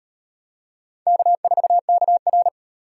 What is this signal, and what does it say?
Morse code, heard as a keyed tone, about 33 wpm, 700 Hz. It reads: K4KR